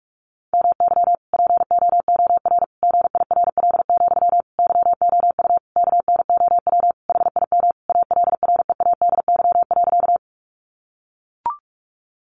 Morse code, heard as a keyed tone, about 44 wpm, 700 Hz. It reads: MY POOR GIRL, YOU KNOW HIM ALREADY.
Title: Sentences from Top 1000 Words 45wpm